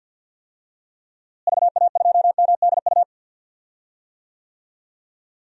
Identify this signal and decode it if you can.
Morse code, heard as a keyed tone, about 48 wpm, 700 Hz. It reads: VA2MDU
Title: DXCC-100 Call Signs - 6 Characters - Repeated 3x - 50wpm